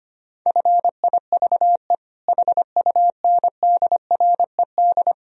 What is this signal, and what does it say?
Morse code, heard as a keyed tone, about 25 wpm, 700 Hz. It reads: FIVE HUNDRED
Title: Sentences from Top 1000 Words - Review Part 3 - 25wpm